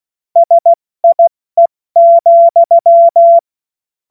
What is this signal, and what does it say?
Morse code, heard as a keyed tone, about 16 wpm, 680 Hz. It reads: SIE,
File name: Morsekurs / Witze / Witz 05. Oktober 2021, 70 F80